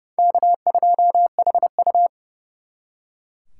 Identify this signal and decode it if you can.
Morse code, heard as a keyed tone, about 39 wpm, 700 Hz. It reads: K2HU